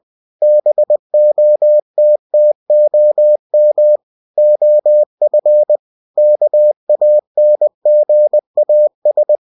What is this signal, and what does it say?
Morse code, heard as a keyed tone, about 20 wpm, 600 Hz. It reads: BOTTOM OF KANGAS
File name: cw-books / winnie-the-pooh / winnie-20wpm-page-0070